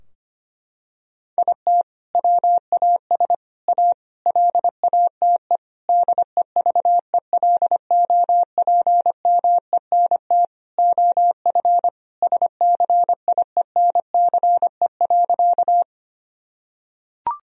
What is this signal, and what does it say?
Morse code, heard as a keyed tone, about 25 wpm, 700 Hz. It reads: IT WAS A LATE DEVELOPMENT OF SCIENCE.
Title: Sentences from Top 1000 Words - Review Part 4 - 25wpm